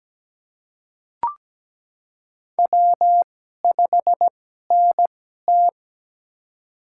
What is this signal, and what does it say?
Morse code, heard as a keyed tone, about 17 wpm, spaced as for 12 wpm, 700 Hz. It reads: W5NT
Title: USA Call Signs - 4 Characters - 2x Character Spacing - 17wpm